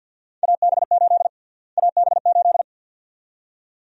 Morse code, heard as a keyed tone, about 50 wpm, 700 Hz. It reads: AB8 AB8
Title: USA Call Sign Prefixes - Repeated 2x - 50wpm